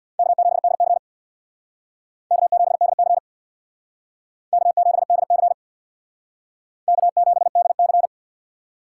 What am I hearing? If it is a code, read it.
Morse code, heard as a keyed tone, about 62 wpm, 700 Hz. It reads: K7DC K7DC K7DC K7DC
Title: Speed Racing - USA 1x2 Call Signs 50wpm